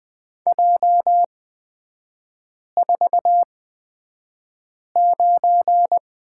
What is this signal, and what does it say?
Morse code, heard as a keyed tone, about 20 wpm, spaced as for 5 wpm, 690 Hz. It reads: J49
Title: 229 Lesson 37e - Letter x - Three Characters 20wpm